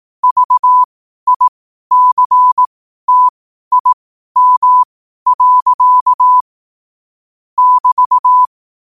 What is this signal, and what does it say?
Morse code, heard as a keyed tone, about 18 wpm, spaced as for 12 wpm, 1000 Hz. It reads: VICTIM. <BT>
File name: qotdfiles / 12wpm